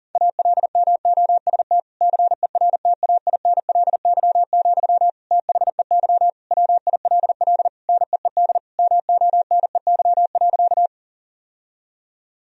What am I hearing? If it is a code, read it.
Morse code, heard as a keyed tone, about 40 wpm, 700 Hz. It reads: ALMOST CERTAINLY, THEY WILL NEED MONEY.
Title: Sentences from Top 1000 Words - Review Part 4 - 40wpm